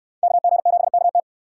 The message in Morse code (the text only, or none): KG7KT